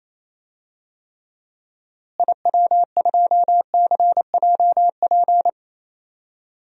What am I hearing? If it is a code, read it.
Morse code, heard as a keyed tone, about 28 wpm, 700 Hz. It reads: IW2CJP
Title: DXCC-100 Call Signs - 6 Characters - Repeated 3x - 28wpm